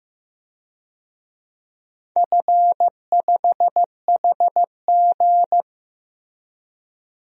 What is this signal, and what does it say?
Morse code, heard as a keyed tone, about 15 wpm, 700 Hz. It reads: F5HG
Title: DXCC-100 Call Signs - 4 Characters - Repeated 3x - 15wpm